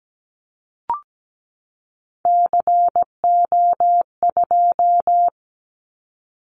Morse code, heard as a keyed tone, about 17 wpm, 700 Hz. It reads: CO2
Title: Callsign Prefixes for Top 100 DXCC - 17wpm